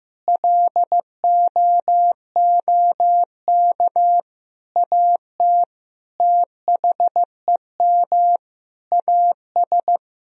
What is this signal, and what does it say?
Morse code, heard as a keyed tone, about 15 wpm, 700 Hz. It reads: LOOK AT THEM AS
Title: Sentences from Top 1000 Words - Review Part 1 - 15wpm